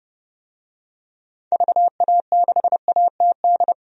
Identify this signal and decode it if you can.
Morse code, heard as a keyed tone, about 32 wpm, 700 Hz. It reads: VA6ATD
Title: DXCC-100 Call Signs - 6 Characters - 30wpm